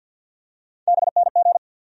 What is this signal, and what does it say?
Morse code, heard as a keyed tone, about 52 wpm, 700 Hz. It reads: BNG
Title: USA Call Sign Postfixes - 3 Letters 50wpm